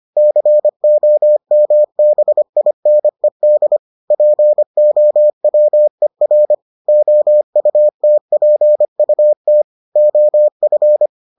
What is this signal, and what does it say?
Morse code, heard as a keyed tone, about 25 wpm, 600 Hz. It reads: COMBINED POWER OUTPUT OF